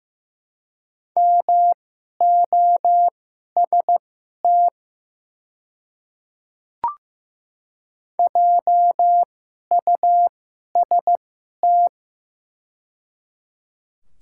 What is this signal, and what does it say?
Morse code, heard as a keyed tone, about 15 wpm, spaced as for 11 wpm, 700 Hz. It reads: MOST JUST
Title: 2x Character Spacing - Top 200 Words - 15wpm